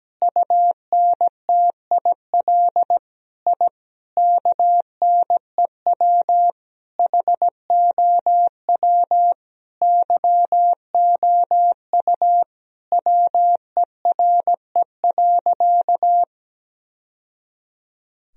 Morse code, heard as a keyed tone, about 17 wpm, 700 Hz. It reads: UNTIL I KNEW HOW YOU WERE.
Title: Sentences from Top 1000 Words - Review Part 1 - 17wpm